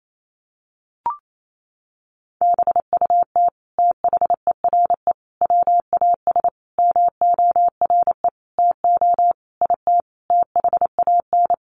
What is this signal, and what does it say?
Morse code, heard as a keyed tone, about 28 wpm, 700 Hz. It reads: BUT THERE WAS MORE TO IT THAN
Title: Sentences from Top 100 Words 28wpm